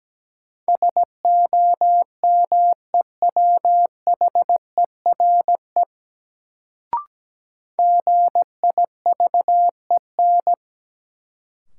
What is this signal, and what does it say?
Morse code, heard as a keyed tone, about 17 wpm, 700 Hz. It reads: SOMEWHERE GIVEN
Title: Sentences from Top 1400 Words - New Word Drill - 17wpm